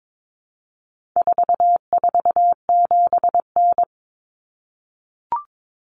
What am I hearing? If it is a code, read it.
Morse code, heard as a keyed tone, about 22 wpm, 700 Hz. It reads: 447N
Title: Common Cut Numbers - 4-Digit 22wpm